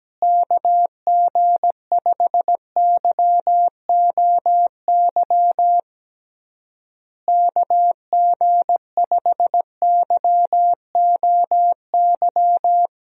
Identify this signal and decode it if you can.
Morse code, heard as a keyed tone, about 17 wpm, 700 Hz. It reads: KG5YOY KG5YOY
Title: USA Call Signs - 6 Characters - Repeated 3x - 17wpm